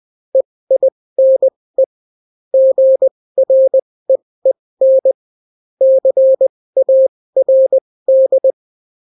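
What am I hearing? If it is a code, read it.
Morse code, heard as a keyed tone, about 20 wpm, spaced as for 16 wpm, 530 Hz. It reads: EINE GREEN CARD
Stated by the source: Daily Morse weather and top stories at 16 wpm, clean without QRN.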